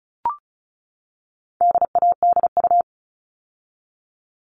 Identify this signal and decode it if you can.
Morse code, heard as a keyed tone, about 34 wpm, 700 Hz. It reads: DADU
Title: All Cut Numbers - 4-Digit 35wpm